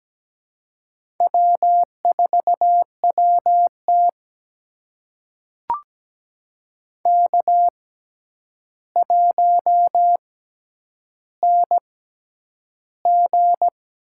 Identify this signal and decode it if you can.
Morse code, heard as a keyed tone, about 17 wpm, spaced as for 9 wpm, 700 Hz. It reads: W4WT K1NG
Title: USA Call Signs - 4 Characters - 6x Character Spacing - 17wpm